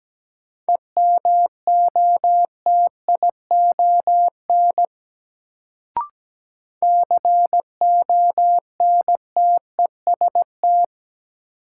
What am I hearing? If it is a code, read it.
Morse code, heard as a keyed tone, about 17 wpm, 700 Hz. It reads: EMOTION CONTEST
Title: Sentences from Top 1600 Words - New Word Drill - 17wpm